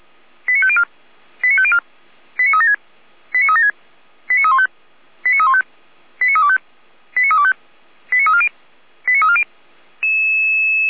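Five-tone selective call (ZVEI), heard as quick runs of five tones, tones of about 70 ms. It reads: 89504 89504 89376 89376 89315 89315 89325 89325 89340 89340
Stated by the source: Five-tone ZVEI selective call received by SDR, logged on the Feuerwehr Orlamünde audio server.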